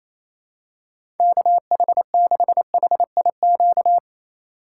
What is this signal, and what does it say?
Morse code, heard as a keyed tone, about 28 wpm, 700 Hz. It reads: KH6HIQ
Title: USA Call Signs - 4x Character Spacing - 28wpm